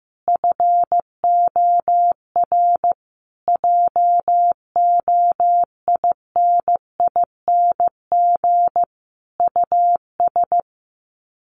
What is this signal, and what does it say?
Morse code, heard as a keyed tone, about 15 wpm, 700 Hz. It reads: FOR JOINING US
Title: Sets of 5 Words 15wpm